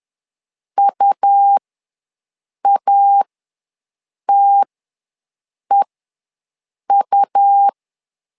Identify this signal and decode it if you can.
Morse code, heard as a keyed tone, about 11 wpm, spaced as for 6 wpm, 800 Hz. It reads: UATEU
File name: LEZIONI / 2 UHDN / 2 U ws 10wpm 2017